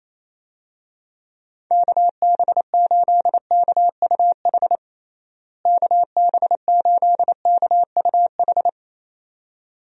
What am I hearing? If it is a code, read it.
Morse code, heard as a keyed tone, about 28 wpm, 700 Hz. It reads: KB8KUH KB8KUH
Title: USA Call Signs - 6 Characters - Repeated 2x - 28wpm